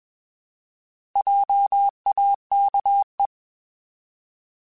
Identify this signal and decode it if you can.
Morse code, heard as a keyed tone, about 21 wpm, 800 Hz. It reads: JAKE
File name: Alphabetical / Alphabetical mp3s / Js @20